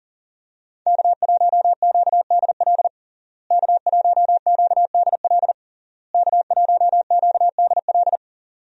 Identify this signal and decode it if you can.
Morse code, heard as a keyed tone, about 40 wpm, 700 Hz. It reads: K1QDL K1QDL K1QDL
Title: USA Call Signs - 5 Characters - Repeated 3x - 40wpm